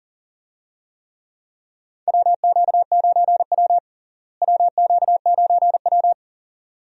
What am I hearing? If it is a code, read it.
Morse code, heard as a keyed tone, about 40 wpm, 700 Hz. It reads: WQ9W WQ9W
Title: USA Call Signs - 4 Characters - Repeated 2x - 40wpm